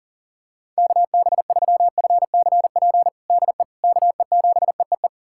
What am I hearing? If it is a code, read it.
Morse code, heard as a keyed tone, about 40 wpm, 700 Hz. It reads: KB3FCP DE KE7EEE
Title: USA Callsign DE Callsign 40wpm